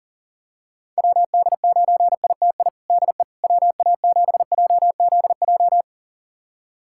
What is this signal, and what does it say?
Morse code, heard as a keyed tone, about 40 wpm, 700 Hz. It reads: WD9ITI DE WA7JZJ
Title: USA Callsign DE Callsign - 6 Characters - 40wpm